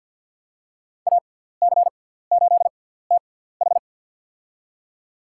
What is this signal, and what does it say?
Morse code, heard as a keyed tone, about 48 wpm, spaced as for 17 wpm, 700 Hz. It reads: AC8TH
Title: USA Call Signs - 5 Characters - 6x Character Spacing - 50wpm